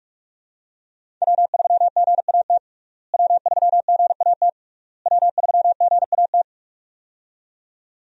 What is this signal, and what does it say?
Morse code, heard as a keyed tone, about 46 wpm, 700 Hz. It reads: W3GAT W3GAT W3GAT
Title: USA Call Signs - 5 Characters - Repeated 3x - 45wpm